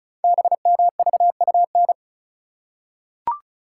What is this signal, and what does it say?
Morse code, heard as a keyed tone, about 36 wpm, 700 Hz. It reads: BMVUN
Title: Code Groups - 5 Letters - 35wpm